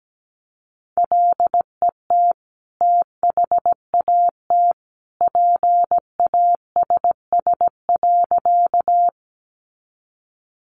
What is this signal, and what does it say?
Morse code, heard as a keyed tone, about 17 wpm, 700 Hz. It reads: LET THAT PASS.
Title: Sentences from Top 400 Words 17wpm